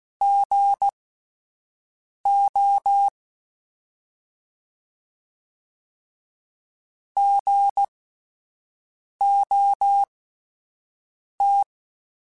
Morse code, heard as a keyed tone, about 16 wpm, spaced as for 5 wpm, 800 Hz. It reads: GO GOT
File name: Alphabetical / Alphabetical mp3s / Gs @5